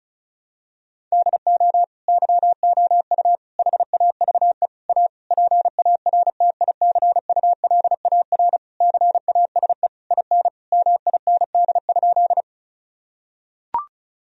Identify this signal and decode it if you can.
Morse code, heard as a keyed tone, about 36 wpm, 700 Hz. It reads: DO YOU HAVE A PARTICULAR CASE IN MIND?
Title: Sentences from Top 1000 Words - Review Part 4 - 35wpm